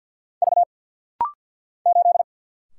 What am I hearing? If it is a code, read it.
Morse code, heard as a keyed tone, about 50 wpm, 700 Hz. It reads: V 8
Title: Single Letter-Number - Rapid-Fire 50wpm